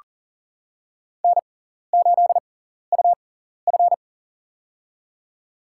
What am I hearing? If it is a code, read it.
Morse code, heard as a keyed tone, about 40 wpm, spaced as for 14 wpm, 700 Hz. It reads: N8UF
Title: USA Call Signs - 4 Characters - 6x Character Spacing - 40wpm